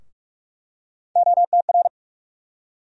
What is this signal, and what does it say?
Morse code, heard as a keyed tone, about 44 wpm, 700 Hz. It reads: OTR
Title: USA Call Sign Postfixes - 3 Letters 45wpm